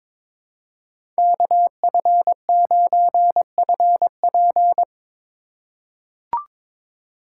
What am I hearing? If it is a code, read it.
Morse code, heard as a keyed tone, about 22 wpm, 700 Hz. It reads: KF9FP
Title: USA Call Signs - 5 Characters - 4x Character Spacing - 22wpm